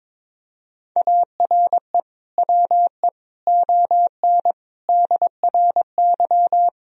Morse code, heard as a keyed tone, about 22 wpm, 700 Hz. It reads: ARE WE ON DRY
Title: Sentences from Top 1000 Words - Review Part 3 - 22wpm